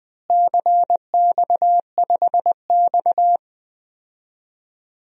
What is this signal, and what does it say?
Morse code, heard as a keyed tone, about 20 wpm, 700 Hz. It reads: CX5X
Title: DXCC-100 Call Signs - 4 Characters - Repeated 2x - 20wpm